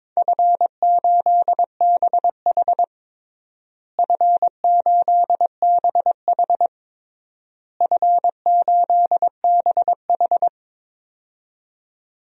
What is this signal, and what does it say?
Morse code, heard as a keyed tone, about 22 wpm, 700 Hz. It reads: F8BH F8BH F8BH
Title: DXCC-100 Call Signs - 4 Characters - Repeated 3x - 22wpm